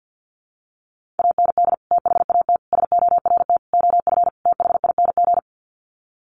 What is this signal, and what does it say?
Morse code, heard as a keyed tone, about 50 wpm, 700 Hz. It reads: AND THAT SORT OF THING
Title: Sets of 5 Words Encore2 50wpm